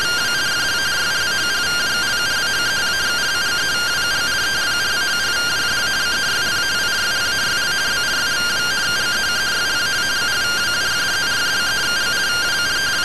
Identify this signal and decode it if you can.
Two-tone teleprinter signal, about 170 Hz shift.